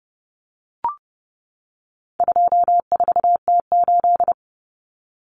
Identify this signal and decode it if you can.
Morse code, heard as a keyed tone, about 30 wpm, 700 Hz. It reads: 24T8
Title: Common Cut Numbers - 4-Digit 30wpm